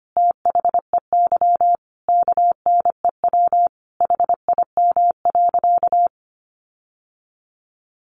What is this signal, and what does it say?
Morse code, heard as a keyed tone, about 25 wpm, 700 Hz. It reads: THEY KNEW HIM.
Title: Sentences from Top 1500 Words 25wpm